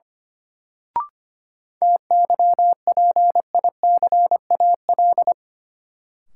Morse code, heard as a keyed tone, about 25 wpm, 700 Hz. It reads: TYPICAL
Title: Sentences from Top 1100 Words - New Word Drill Alt - 25wpm